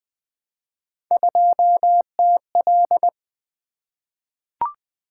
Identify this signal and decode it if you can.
Morse code, heard as a keyed tone, about 20 wpm, 700 Hz. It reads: 2TL
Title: Three Letter-Number Combinations - 20wpm